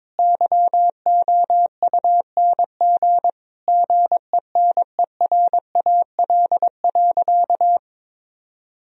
Morse code, heard as a keyed tone, about 22 wpm, 700 Hz. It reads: YOUNG GENERAL.
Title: Sentences from Top 1000 Words - Review Part 3 - 22wpm